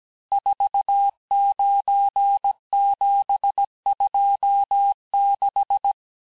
Morse code, heard as a keyed tone, about 17 wpm, 800 Hz. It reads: 49726
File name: corrige-cours-1581-texte-chiffre-vitesse-16-mpm